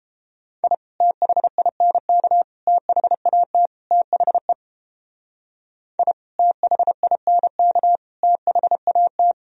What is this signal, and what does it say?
Morse code, heard as a keyed tone, about 29 wpm, 700 Hz. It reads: I THINK THAT THE I THINK THAT